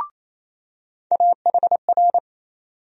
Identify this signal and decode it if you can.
Morse code, heard as a keyed tone, about 28 wpm, 700 Hz. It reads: AHR
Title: USA Call Sign Postfixes - 3 Letters 28wpm